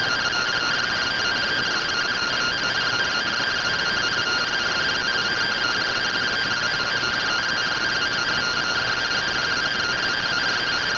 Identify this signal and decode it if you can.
Two-tone teleprinter signal, about 170 Hz shift.